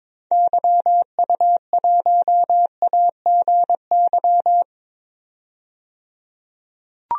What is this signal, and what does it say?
Morse code, heard as a keyed tone, about 21 wpm, 700 Hz. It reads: YU1AGY
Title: DXCC-100 Call Signs - 6 Characters - 22wpm